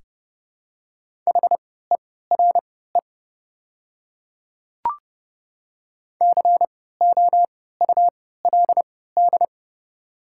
Morse code, heard as a keyed tone, about 30 wpm, spaced as for 17 wpm, 690 Hz. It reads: HERE COULD
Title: 3x Character Spacing - Top 200 Words - 30wpm